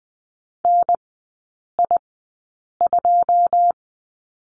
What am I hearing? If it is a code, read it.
Morse code, heard as a keyed tone, about 20 wpm, spaced as for 8 wpm, 690 Hz. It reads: NI2
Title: 081 Lesson 14e - Number 2 - Three Characters 20wpm